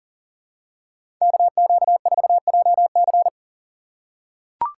Morse code, heard as a keyed tone, about 40 wpm, 700 Hz. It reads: KQ4JC
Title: USA Call Signs - 5 Characters - 4x Character Spacing - 40wpm